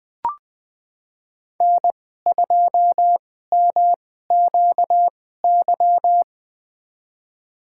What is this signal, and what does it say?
Morse code, heard as a keyed tone, about 21 wpm, spaced as for 15 wpm, 700 Hz. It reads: N2MQY
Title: USA Call Signs - 5 Characters - 2x Character Spacing - 20wpm